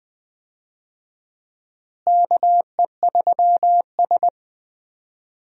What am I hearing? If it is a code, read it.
Morse code, heard as a keyed tone, about 20 wpm, 700 Hz. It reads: KE3S
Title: USA Call Signs - 4 Characters - Repeated 2x - 20wpm